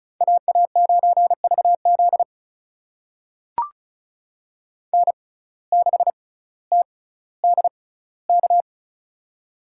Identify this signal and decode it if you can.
Morse code, heard as a keyed tone, about 36 wpm, spaced as for 18 wpm, 700 Hz. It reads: AA9VZ N6TDK
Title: USA Call Signs - 5 Characters - 6x Character Spacing - 35wpm